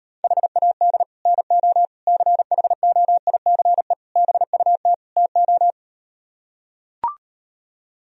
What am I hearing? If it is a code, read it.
Morse code, heard as a keyed tone, about 38 wpm, 700 Hz. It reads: HAD NO CHOICE BUT TO